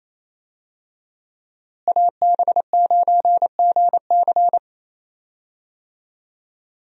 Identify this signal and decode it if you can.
Morse code, heard as a keyed tone, about 28 wpm, 700 Hz. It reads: AB9GC
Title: USA Call Signs - 5 Characters - Repeated 2x - 28wpm